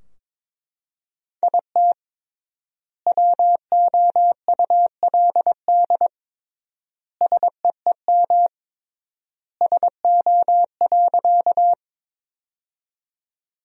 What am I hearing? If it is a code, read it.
Morse code, heard as a keyed tone, about 21 wpm, 700 Hz. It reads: IT WOULD SEEM SO.
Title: 3x Word-Spacing - Sentences from Top 300 Words - 22wpm